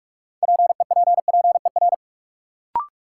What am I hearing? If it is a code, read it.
Morse code, heard as a keyed tone, about 46 wpm, 700 Hz. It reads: PEPPER
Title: Sets of 5 Words - New Word Drill - 45wpm